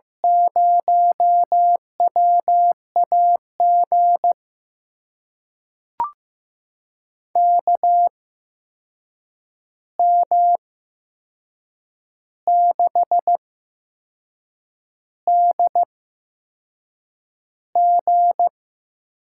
Morse code, heard as a keyed tone, about 15 wpm, spaced as for 6 wpm, 700 Hz. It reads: 0WAG KM6DG
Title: USA Call Signs - 6 Characters - 8x Character Spacing - 15wpm